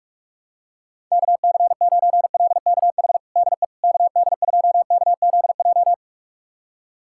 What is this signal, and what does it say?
Morse code, heard as a keyed tone, about 45 wpm, 700 Hz. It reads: KC9LKH DE KD2KZJ
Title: USA Callsign DE Callsign 45wpm